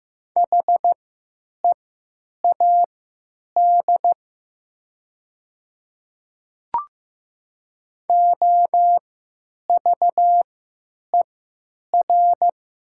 Morse code, heard as a keyed tone, about 15 wpm, spaced as for 9 wpm, 700 Hz. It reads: HEAD OVER
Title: 3x Character Spacing - Top 500 Words - 15wpm